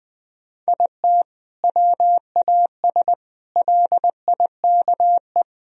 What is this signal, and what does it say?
Morse code, heard as a keyed tone, about 20 wpm, 700 Hz. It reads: IT WAS LIKE
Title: Sentences from Top 1000 Words - Review Part 3 - 20wpm